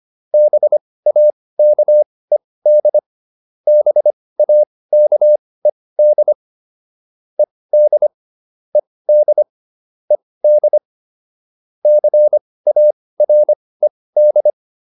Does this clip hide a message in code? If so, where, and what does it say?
Yes: Morse code from 0.3 to 14.5 s, heard as a keyed tone, about 25 wpm, spaced as for 18 wpm, 600 Hz. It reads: BAKED BAKED ED ED ED CARED